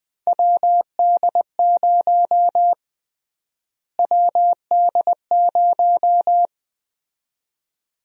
Morse code, heard as a keyed tone, about 21 wpm, 700 Hz. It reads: WD0 WD0
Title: USA Call Sign Prefixes - Repeated 2x - 20wpm